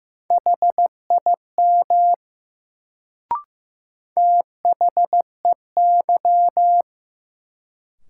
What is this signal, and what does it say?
Morse code, heard as a keyed tone, about 15 wpm, 700 Hz. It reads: HIM THEY
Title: Sentences from Top 100 Words - New Word Drill - 15wpm